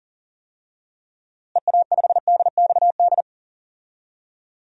Morse code, heard as a keyed tone, about 40 wpm, 700 Hz. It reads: EA5DXD
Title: DXCC-100 Call Signs - 6 Characters - Repeated 3x - 40wpm